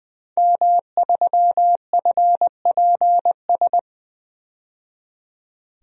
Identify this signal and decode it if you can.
Morse code, heard as a keyed tone, about 20 wpm, 700 Hz. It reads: M3FPS
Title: DXCC-100 Call Signs - 5 Characters - 20wpm